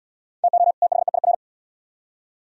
Morse code, heard as a keyed tone, about 72 wpm, 700 Hz. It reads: TO THEIR